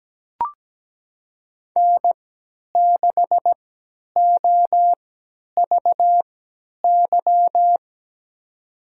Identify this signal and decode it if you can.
Morse code, heard as a keyed tone, about 17 wpm, spaced as for 10 wpm, 700 Hz. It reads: N6OVY
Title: USA Call Signs - 5 Characters - 3x Character Spacing - 17wpm